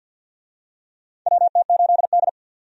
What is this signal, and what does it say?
Morse code, heard as a keyed tone, about 50 wpm, 700 Hz. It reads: WT8D
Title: USA Call Signs - 4 Characters - 50wpm